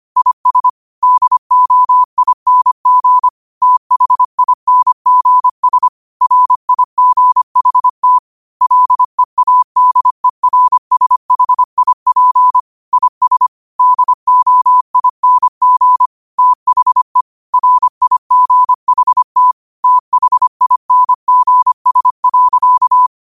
Morse code, heard as a keyed tone, about 25 wpm, 1000 Hz. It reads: IS DOING THINGS RIGHT LEADERSHIP IS DOING THE RIGHT THINGS.
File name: qotdfiles / 25wpm